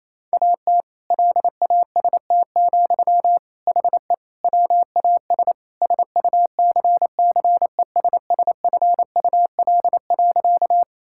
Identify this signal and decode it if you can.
Morse code, heard as a keyed tone, about 28 wpm, 700 Hz. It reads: AT LAST, HE WAS SUCCESSFUL.
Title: Sentences from Top 1000 Words - Review Part 6 - 28wpm